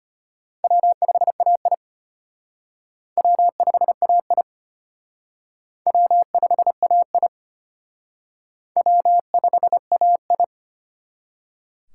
Morse code, heard as a keyed tone, about 30 wpm, 700 Hz. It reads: W5AI W5AI W5AI W5AI
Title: Speed Racing - USA 1x2 Call Signs 25wpm